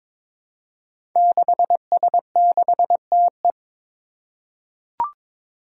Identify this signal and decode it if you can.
Morse code, heard as a keyed tone, about 22 wpm, 700 Hz. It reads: 6S6TE
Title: Five Challenging Letter-Number Combinations 22wpm